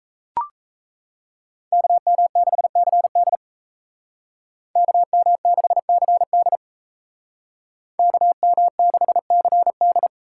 Morse code, heard as a keyed tone, about 38 wpm, 700 Hz. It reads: KM6CD KM6CD KM6CD